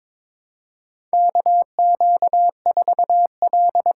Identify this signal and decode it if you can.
Morse code, heard as a keyed tone, about 22 wpm, 700 Hz. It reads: KQ4L
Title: USA Call Signs - 5 Characters - 8x Character Spacing - 22wpm